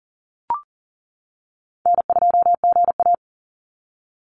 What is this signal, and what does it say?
Morse code, heard as a keyed tone, about 40 wpm, 700 Hz. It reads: N2GA
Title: Call Signs - 40wpm